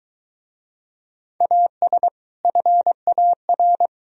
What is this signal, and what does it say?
Morse code, heard as a keyed tone, about 26 wpm, 700 Hz. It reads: AS FAR